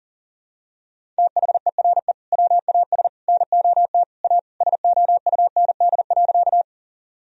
Morse code, heard as a keyed tone, about 37 wpm, 700 Hz. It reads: THERE WAS NOT A SOUND.
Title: Sentences from Top 1000 Words - Review Part 1 - 40wpm